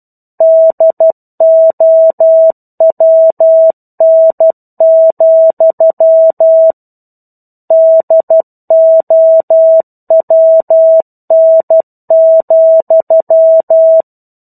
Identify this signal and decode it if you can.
Morse code, heard as a keyed tone, about 12 wpm, 650 Hz. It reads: DOWN, DOWN,
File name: 01-Chapter-01 12 WPM